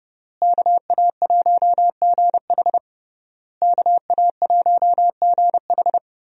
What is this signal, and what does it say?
Morse code, heard as a keyed tone, about 32 wpm, 700 Hz. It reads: KA1GH KA1GH
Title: USA Call Signs - 5 Characters - Repeated 3x - 30wpm